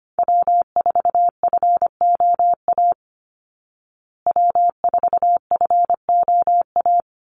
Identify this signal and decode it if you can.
Morse code, heard as a keyed tone, about 25 wpm, 700 Hz. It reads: W4FOA W4FOA
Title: Call Signs Repeated 3x - 25wpm